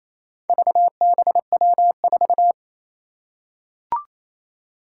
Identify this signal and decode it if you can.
Morse code, heard as a keyed tone, about 28 wpm, 700 Hz. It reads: VBW4
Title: Four Challenging Letter-Number Combinations 28wpm